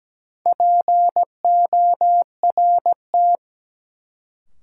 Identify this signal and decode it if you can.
Morse code, heard as a keyed tone, about 17 wpm, 700 Hz. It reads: PORT
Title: Sentences from Top 1000 Words - Review Part 1 - New Word Drill - 17wpm